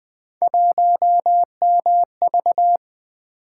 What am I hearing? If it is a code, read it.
Morse code, heard as a keyed tone, about 20 wpm, 700 Hz. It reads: 1MV
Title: Three Letter-Number Combinations - 20wpm